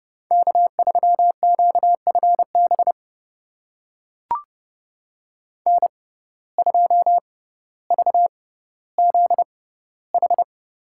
Morse code, heard as a keyed tone, about 30 wpm, spaced as for 15 wpm, 700 Hz. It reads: K3QFB N2VZH
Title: USA Call Signs - 5 Characters - 6x Character Spacing - 30wpm